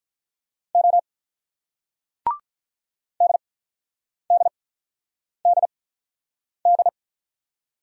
Morse code, heard as a keyed tone, about 48 wpm, spaced as for 9 wpm, 700 Hz. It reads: O DDDD